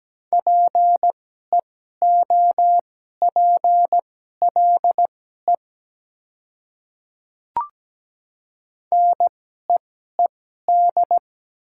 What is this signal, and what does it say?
Morse code, heard as a keyed tone, about 17 wpm, spaced as for 12 wpm, 700 Hz. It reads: PEOPLE NEED
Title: 2x Character Spacing - Top 300 Words - 17wpm